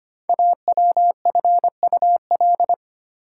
Morse code, heard as a keyed tone, about 25 wpm, 700 Hz. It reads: AWFUL
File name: 5 Letter Words - First 3 Letters Given - 25wpm